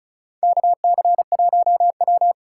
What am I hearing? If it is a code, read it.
Morse code, heard as a keyed tone, about 36 wpm, 700 Hz. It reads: KC1W
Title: USA Call Signs - 2x Character Spacing - 35wpm